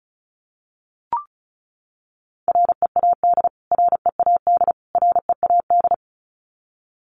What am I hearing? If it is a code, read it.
Morse code, heard as a keyed tone, about 34 wpm, 700 Hz. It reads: READ READ READ
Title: Top 500 Words Repeated 35wpm